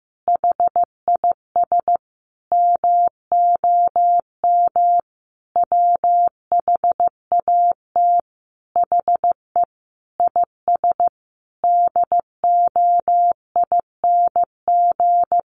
Morse code, heard as a keyed tone, about 15 wpm, 700 Hz. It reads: HIS MOM WHAT HE IS DOING
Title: Easy Phrases 15wpm